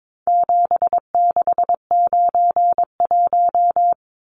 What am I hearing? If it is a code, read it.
Morse code, heard as a keyed tone, about 22 wpm, 700 Hz. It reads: 7691
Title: Numbers - 4-Digit 22wpm